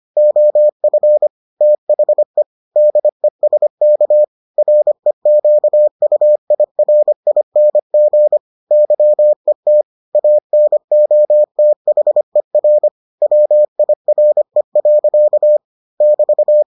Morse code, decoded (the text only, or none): OF THE DESK REQUIRING YET ANOTHER WIRE. <BT>